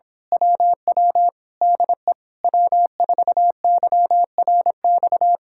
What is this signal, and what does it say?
Morse code, heard as a keyed tone, about 26 wpm, 700 Hz. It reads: WW DE W4YRX
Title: Speed-Racing Common QSO Phrases - 22wpm